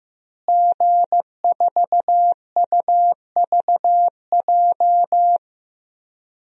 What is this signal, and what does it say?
Morse code, heard as a keyed tone, about 15 wpm, 700 Hz. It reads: G4UVJ
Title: DXCC-100 Call Signs - 5 Characters - Repeated 3x - 15wpm